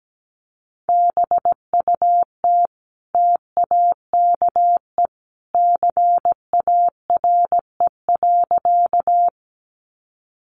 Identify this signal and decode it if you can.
Morse code, heard as a keyed tone, about 17 wpm, 700 Hz. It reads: BUT TAKE CARE.
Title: Sentences from Top 300 Words 17wpm